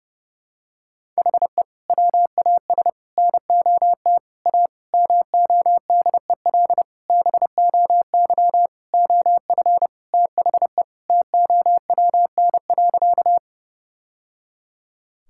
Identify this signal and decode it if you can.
Morse code, heard as a keyed tone, about 30 wpm, 700 Hz. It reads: HE WAS NOT A MODEL BOY OF THE TOWN.
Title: Sentences from Top 1000 Words - Review Part 5 - 30wpm